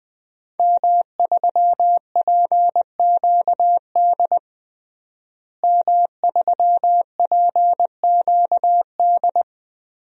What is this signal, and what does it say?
Morse code, heard as a keyed tone, about 20 wpm, 700 Hz. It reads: M3PQD M3PQD
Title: DXCC-100 Call Signs - 5 Characters - Repeated 3x - 20wpm